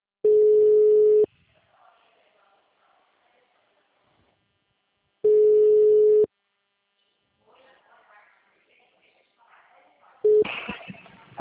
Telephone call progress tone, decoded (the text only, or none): ringback tone